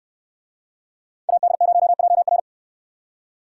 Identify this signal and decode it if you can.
Morse code, heard as a keyed tone, about 64 wpm, 700 Hz. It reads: AD0JR